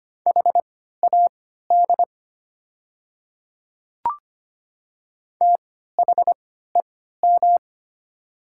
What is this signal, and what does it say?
Morse code, heard as a keyed tone, about 25 wpm, spaced as for 14 wpm, 700 Hz. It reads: HAD THEM
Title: 3x Character Spacing - Top 100 Words - 25wpm